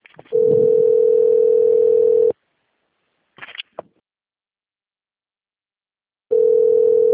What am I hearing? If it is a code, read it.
Ringback tone (North America), 440 + 480 Hz.
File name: rg-1600-unknown